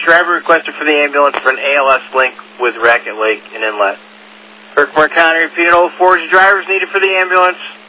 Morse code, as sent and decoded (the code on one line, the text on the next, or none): none